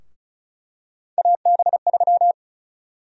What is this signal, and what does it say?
Morse code, heard as a keyed tone, about 34 wpm, 700 Hz. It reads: AB3
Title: USA Call Sign Prefixes - 35wpm